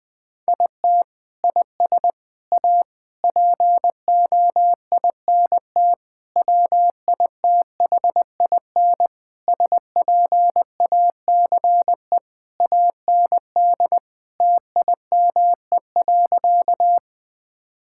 Morse code, heard as a keyed tone, about 20 wpm, 700 Hz. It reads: IT IS A POINT WITHIN SPACE AND TIME.
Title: Sentences from Top 1000 Words - Review Part 5 - 20wpm